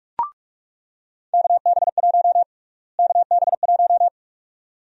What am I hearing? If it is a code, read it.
Morse code, heard as a keyed tone, about 46 wpm, 700 Hz. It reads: KB1 KB1
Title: USA Call Sign Prefixes - Repeated 2x - 45wpm